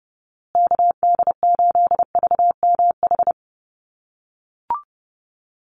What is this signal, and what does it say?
Morse code, heard as a keyed tone, about 30 wpm, 700 Hz. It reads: KD8VMH
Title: USA Call Signs Repeated 3x 30wpm